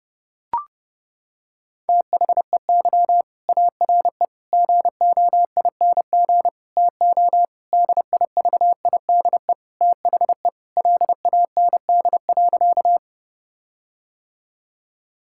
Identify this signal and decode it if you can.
Morse code, heard as a keyed tone, about 30 wpm, 700 Hz. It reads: THEY ARE GOING TO DIVIDE THE LAND.
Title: Sentences from Top 1600 Words 30wpm